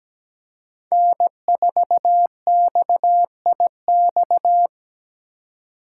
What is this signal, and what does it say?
Morse code, heard as a keyed tone, about 17 wpm, 700 Hz. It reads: N4XIX
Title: USA Call Signs - 5 Characters - 3x Character Spacing - 17wpm